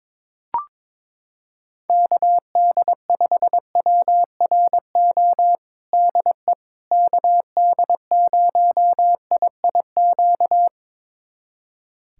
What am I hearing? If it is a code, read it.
Morse code, heard as a keyed tone, about 21 wpm, 700 Hz. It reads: KD5WRO DE KD0IIQ
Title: USA Callsign DE Callsign - 6 Characters - 22wpm